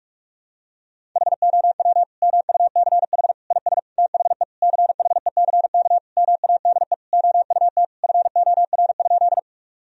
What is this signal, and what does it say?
Morse code, heard as a keyed tone, about 46 wpm, 700 Hz. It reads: HOW MUCH IS THE CHECK MADE OUT FOR?
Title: Sentences from Top 1000 Words - Review Part 3 - 45wpm